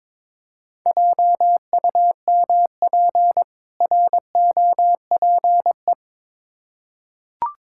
Morse code, heard as a keyed tone, about 22 wpm, 690 Hz. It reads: JUMP ROPE
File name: Difficult Words with Hints - 22wpm